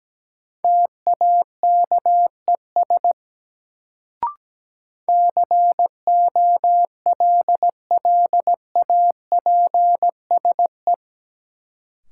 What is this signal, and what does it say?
Morse code, heard as a keyed tone, about 17 wpm, 700 Hz. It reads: TAKES COLLAPSE
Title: Sets of 5 Words Encore - New Word Drill - 17wpm